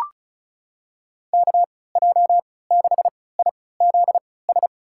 Morse code, heard as a keyed tone, about 34 wpm, spaced as for 20 wpm, 700 Hz. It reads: KJ6IZS
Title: USA Call Signs - 6 Characters - 3x Character Spacing - 35wpm